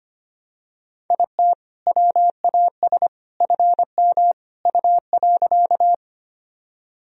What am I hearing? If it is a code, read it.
Morse code, heard as a keyed tone, about 25 wpm, 700 Hz. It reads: IT WAS FM U.
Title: Sentences from Top 100 Words and Abbreviations 25wpm